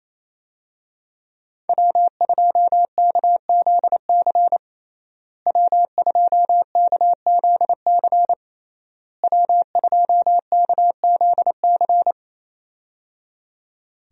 Morse code, heard as a keyed tone, about 28 wpm, 700 Hz. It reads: W2KZC W2KZC W2KZC
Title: USA Call Signs - 5 Characters - Repeated 3x - 28wpm